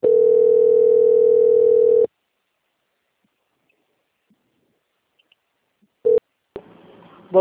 Ringback tone (North America), 440 + 480 Hz.
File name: out- - - - - .18711